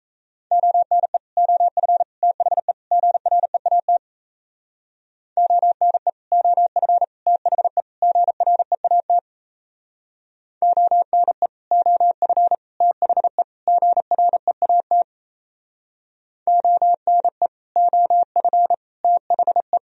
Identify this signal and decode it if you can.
Morse code, heard as a keyed tone, about 34 wpm, 700 Hz. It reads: ONE OF THE GREAT ONE OF THE GREAT ONE OF THE GREAT ONE OF THE
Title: Speed-Racing - Sets of 4 Words 28wpm